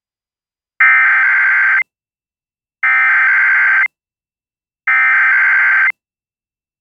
Emergency Alert System (SAME) tones (header bursts), heard as harsh buzzing bursts. It is heard from 0.8 to 5.9 s.